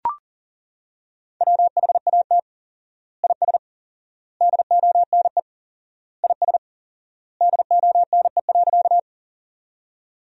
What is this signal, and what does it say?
Morse code, heard as a keyed tone, about 38 wpm, 700 Hz. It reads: WHAT IS DONE IS DONE.
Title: 4x Word-Spacing - Sentences from Top 500 Words - 40wpm